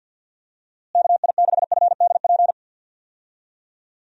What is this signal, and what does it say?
Morse code, heard as a keyed tone, about 51 wpm, 700 Hz. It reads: KI6FDP
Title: USA Call Signs - 6 Characters - Repeated 3x - 50wpm